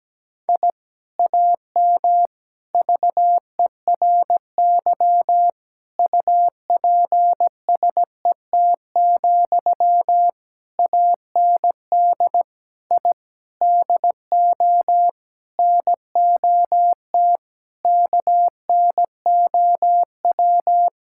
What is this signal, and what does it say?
Morse code, heard as a keyed tone, about 17 wpm, 700 Hz. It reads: I AM VERY UPSET, AND I DO NOT KNOW
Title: Sentences from Top 1000 Words - Review Part 5 - 17wpm